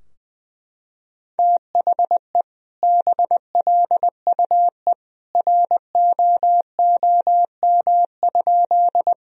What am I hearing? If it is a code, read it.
Morse code, heard as a keyed tone, about 20 wpm, 700 Hz. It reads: THE BLUE ROOM?
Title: Sentences from Top 1000 Words - Review Part 3 - 20wpm